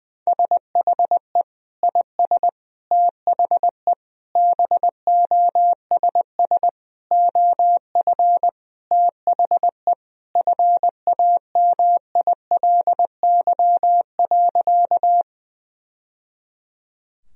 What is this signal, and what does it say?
Morse code, heard as a keyed tone, about 20 wpm, 700 Hz. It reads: SHE IS THE BOSS OF THE FAMILY.
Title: Sentences from Top 1000 Words - Review Part 4 - 20wpm